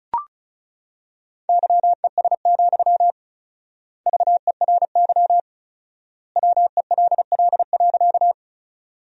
Morse code, heard as a keyed tone, about 36 wpm, 700 Hz. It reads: YES, VERY WELL.
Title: 4x Word-Spacing - Sentences from Top 500 Words - 35wpm